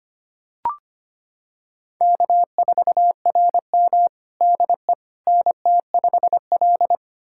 Morse code, heard as a keyed tone, about 25 wpm, 700 Hz. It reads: K4RM DE NT5L
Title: USA Callsign DE Callsign - 4 Characters - 25wpm